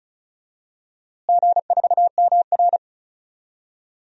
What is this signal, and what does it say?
Morse code, heard as a keyed tone, about 36 wpm, 700 Hz. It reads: G4MR
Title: DXCC-100 Call Signs - 4 Characters - Repeated 2x - 35wpm